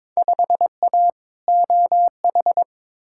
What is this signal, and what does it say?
Morse code, heard as a keyed tone, about 22 wpm, 700 Hz. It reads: 5A OH
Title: Field Day Exchanges Encore - 22wpm